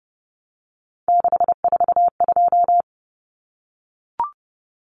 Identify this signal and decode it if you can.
Morse code, heard as a keyed tone, about 30 wpm, 700 Hz. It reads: 642
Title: Numbers - 3-Digit 30wpm